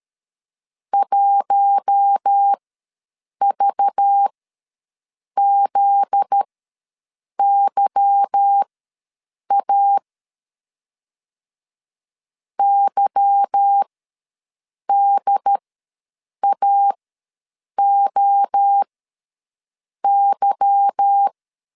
Morse code, heard as a keyed tone, about 13 wpm, spaced as for 7 wpm, 800 Hz. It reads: 1VZYA YDAOY
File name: LEZIONI / 8 YJ45 / 8 Y ws 12wpm 3409